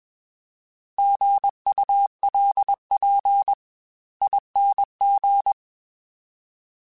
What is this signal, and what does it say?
Morse code, heard as a keyed tone, about 21 wpm, spaced as for 18 wpm, 800 Hz. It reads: GULPING
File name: Alphabetical / Alphabetical mp3s / Gs @20